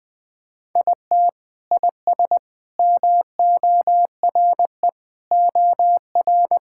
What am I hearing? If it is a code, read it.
Morse code, heard as a keyed tone, about 21 wpm, 700 Hz. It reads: IT IS MORE OR
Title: Sentences from Top 1000 Words - Review Part 6 - 20wpm